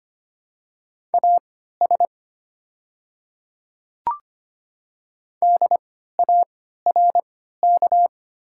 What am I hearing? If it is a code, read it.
Morse code, heard as a keyed tone, about 25 wpm, spaced as for 14 wpm, 700 Hz. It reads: AS DARK
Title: 3x Character Spacing - Top 500 Words - 25wpm